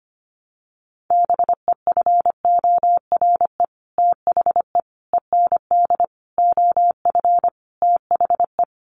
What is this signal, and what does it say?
Morse code, heard as a keyed tone, about 25 wpm, 690 Hz. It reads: BEFORE THE END OF THE
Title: Sets of 5 Words Encore 25wpm